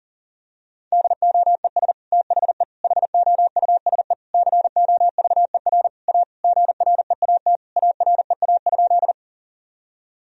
Morse code, heard as a keyed tone, about 37 wpm, 700 Hz. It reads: DOES THE HOUSE COVER A GREAT AREA?
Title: Sentences from Top 1000 Words - Review Part 2 - 40wpm